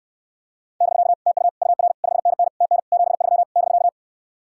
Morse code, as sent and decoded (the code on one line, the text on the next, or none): -...-.- / - ..- / ..- .-. / ..... -. -. / - -. / --... ...-- / -...-.-
<BK> TU UR 5NN TN 73 <BK>